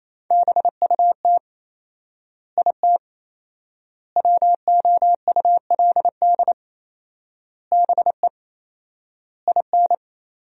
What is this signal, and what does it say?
Morse code, heard as a keyed tone, about 28 wpm, 700 Hz. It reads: BUT IT WOULD BE IN
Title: 4x Word-Spacing - Sentences from Top 300 Words - 28wpm